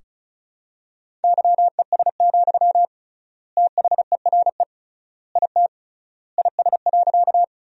Morse code, heard as a keyed tone, about 36 wpm, 700 Hz. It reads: YES, THERE IT IS.
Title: 3x Word-Spacing - Sentences from Top 500 Words - 35wpm